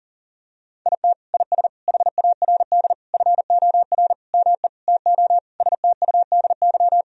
Morse code, heard as a keyed tone, about 40 wpm, 700 Hz. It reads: IT IS HARD FOR ME TO STUDY
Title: Sentences from Top 1000 Words - Review Part 3 - 40wpm